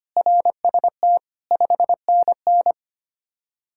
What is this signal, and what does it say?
Morse code, heard as a keyed tone, about 30 wpm, 700 Hz. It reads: RST 5NN
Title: Speed-Racing Common QSO Phrases - 25wpm